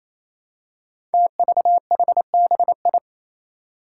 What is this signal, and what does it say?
Morse code, heard as a keyed tone, about 28 wpm, 700 Hz. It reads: TVHBI
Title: Five Challenging Letter-Number Combinations 28wpm